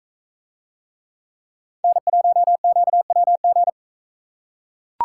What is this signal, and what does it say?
Morse code, heard as a keyed tone, about 39 wpm, 700 Hz. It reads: N1QWG